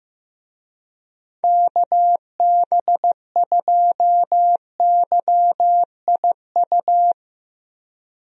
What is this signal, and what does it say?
Morse code, heard as a keyed tone, about 15 wpm, 700 Hz. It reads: KB2YIU
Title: USA Call Signs - 6 Characters - Repeated 3x - 15wpm